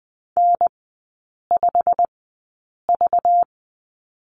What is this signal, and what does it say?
Morse code, heard as a keyed tone, about 20 wpm, spaced as for 8 wpm, 700 Hz. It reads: N5V
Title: 191 Lesson 31e - Letter v - Three Characters 20wpm